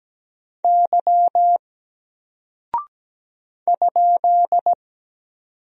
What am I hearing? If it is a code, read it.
Morse code, heard as a keyed tone, about 17 wpm, 700 Hz. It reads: Y ?
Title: Single Character (Letter:Number:Punctuation:Prosign) Alt 17wpm